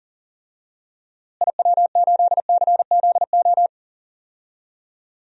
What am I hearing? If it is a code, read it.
Morse code, heard as a keyed tone, about 40 wpm, 700 Hz. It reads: IW8CZO
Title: DXCC-100 Call Signs - 6 Characters - Repeated 2x - 40wpm